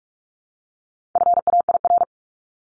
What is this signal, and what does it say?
Morse code, heard as a keyed tone, about 44 wpm, 700 Hz. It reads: FAIR
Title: Top 1251-1500 Words 45wpm